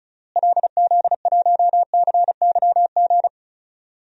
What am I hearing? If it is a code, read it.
Morse code, heard as a keyed tone, about 36 wpm, 700 Hz. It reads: LZ1CYG
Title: DXCC-100 Call Signs - 6 Characters - Repeated 2x - 35wpm